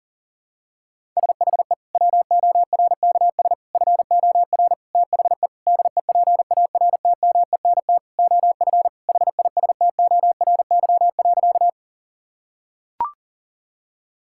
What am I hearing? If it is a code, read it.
Morse code, heard as a keyed tone, about 40 wpm, 700 Hz. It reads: SHE WORKS FOR THE DEPARTMENT OF HISTORY.
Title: Sentences from Top 1000 Words - Review Part 5 - 40wpm